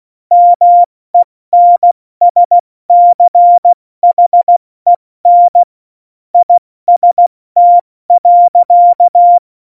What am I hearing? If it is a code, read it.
Morse code, heard as a keyed tone, about 16 wpm, 700 Hz. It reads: MENSCHEN IST.
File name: Morsekurs / Witze / Witz 08. Januar 2019, 70 F80